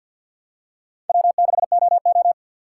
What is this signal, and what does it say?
Morse code, heard as a keyed tone, about 53 wpm, 700 Hz. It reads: W6OO
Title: USA Call Signs - 4 Characters - 6x Character Spacing - 50wpm